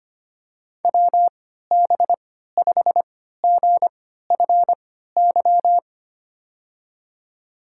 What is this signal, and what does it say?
Morse code, heard as a keyed tone, about 25 wpm, spaced as for 14 wpm, 700 Hz. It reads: WB5GFY
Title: USA Call Signs - 6 Characters - 3x Character Spacing - 25wpm